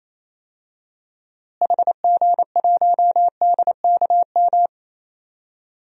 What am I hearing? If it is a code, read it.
Morse code, heard as a keyed tone, about 28 wpm, 700 Hz. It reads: HG1DKM
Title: DXCC-100 Call Signs - 6 Characters - Repeated 3x - 28wpm